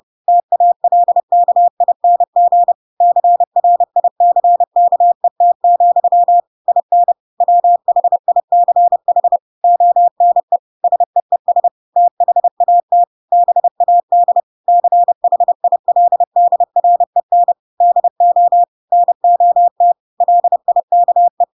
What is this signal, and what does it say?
Morse code, heard as a keyed tone, about 30 wpm, 700 Hz. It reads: TALKING CRICKET, IN WHICH ONE SEES THAT BAD CHILDREN DO NOT LIKE